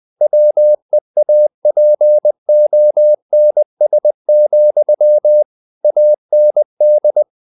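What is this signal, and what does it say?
Morse code, heard as a keyed tone, about 20 wpm, 600 Hz. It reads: WEAPONS, AND
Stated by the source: Morse code at 19 wpm from CW med Gnister, reading Ken Follett's novel Never in English.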